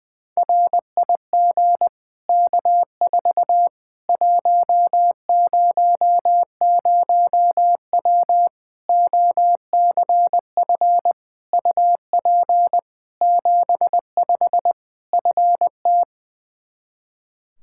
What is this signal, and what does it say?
Morse code, heard as a keyed tone, about 21 wpm, 700 Hz. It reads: RIG K4 100W OCF UP 75 FT
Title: Practice QSO - Exchange 2 - Info - 20wpm